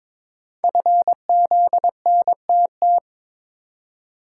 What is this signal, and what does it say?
Morse code, heard as a keyed tone, about 22 wpm, 690 Hz. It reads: FZNTT
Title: Code Groups - 5 Letters - 22wpm